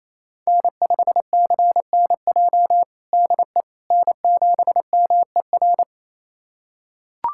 Morse code, heard as a keyed tone, about 28 wpm, 700 Hz. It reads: N5CNJ DE N7MER
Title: USA Callsign DE Callsign - 5 Characters - 28wpm